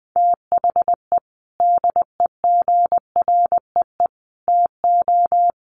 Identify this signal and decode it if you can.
Morse code, heard as a keyed tone, about 20 wpm, 700 Hz. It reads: THE DEGREE TO
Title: Sets of 5 Words Encore2 20wpm